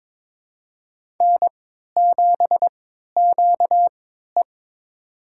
Morse code, heard as a keyed tone, about 22 wpm, spaced as for 12 wpm, 700 Hz. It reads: N7QE
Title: USA Call Signs - 3x Character Spacing - 22wpm